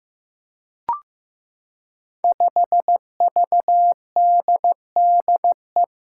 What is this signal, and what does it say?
Morse code, heard as a keyed tone, about 15 wpm, 700 Hz. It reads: 5VDDE
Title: Five Challenging Letter-Number Combinations 15wpm